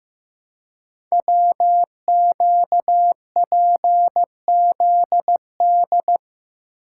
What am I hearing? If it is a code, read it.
Morse code, heard as a keyed tone, about 15 wpm, 700 Hz. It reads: WQPZD
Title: Code Groups - 5 Letter:Numbers - 15wpm